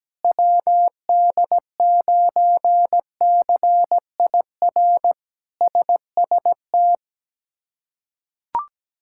Morse code, heard as a keyed tone, about 17 wpm, 700 Hz. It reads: WD9CIR SST
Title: K1USN SST QSO Practice - 17wpm